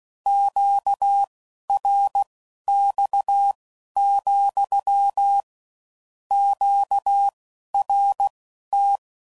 Morse code, heard as a keyed tone, about 16 wpm, spaced as for 11 wpm, 800 Hz. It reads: QRX, QRT
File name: CW Code Groups @15WPM / Q Calls / Q Calls SPSL2